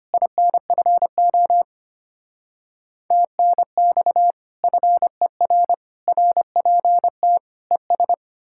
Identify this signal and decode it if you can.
Morse code, heard as a keyed tone, about 33 wpm, 700 Hz. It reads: INFO TNX FER RPT ES